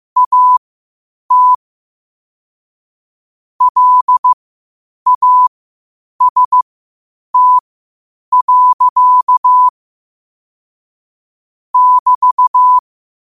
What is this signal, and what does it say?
Morse code, heard as a keyed tone, about 15 wpm, spaced as for 8 wpm, 1000 Hz. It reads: AT LAST. <BT>
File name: qotdfiles / 8wpm